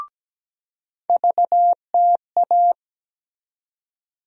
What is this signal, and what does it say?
Morse code, heard as a keyed tone, about 17 wpm, 700 Hz. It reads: VTA
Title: USA Call Sign Postfixes - 3 Letters 17wpm